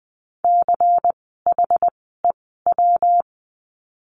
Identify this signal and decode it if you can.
Morse code, heard as a keyed tone, about 20 wpm, 700 Hz. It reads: C H E W
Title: 115 Lesson 19f - Letter w - Words 20wpm